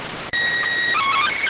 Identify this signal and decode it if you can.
SSTV (slow-scan television), PD 290.